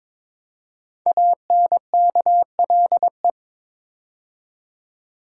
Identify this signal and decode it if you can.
Morse code, heard as a keyed tone, about 22 wpm, 700 Hz. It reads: ANKLE
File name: 5 Letter Words - First 3 Letters Given - 22wpm